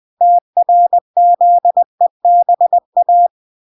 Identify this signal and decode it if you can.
Morse code, heard as a keyed tone, about 20 wpm, 700 Hz. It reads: TRZEBA